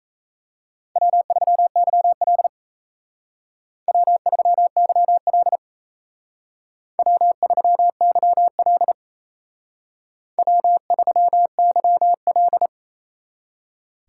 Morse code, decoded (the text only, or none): W3YL W3YL W3YL W3YL